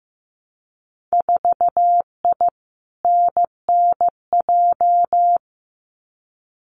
Morse code, heard as a keyed tone, about 15 wpm, 700 Hz. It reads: 4I NNJ
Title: Winter Field Day Exchanges 15wpm